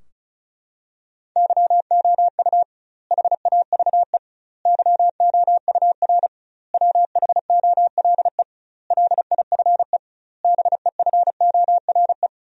Morse code, heard as a keyed tone, about 36 wpm, 700 Hz. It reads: YOU HAVE YOUR WHOLE LIFE BEFORE
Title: 2x Word-Spacing - Sentences from Top 400 Words - 35wpm